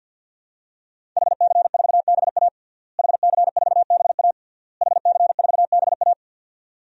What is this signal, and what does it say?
Morse code, heard as a keyed tone, about 49 wpm, 700 Hz. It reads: HC4BA HC4BA HC4BA
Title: DXCC-100 Call Signs - 5 Characters - Repeated 3x - 50wpm